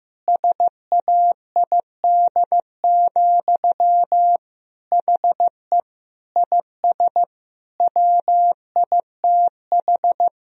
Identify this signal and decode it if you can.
Morse code, heard as a keyed tone, about 15 wpm, 700 Hz. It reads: SAID, HE IS WITH
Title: Sentences from Top 1000 Words - Review Part 3 - 15wpm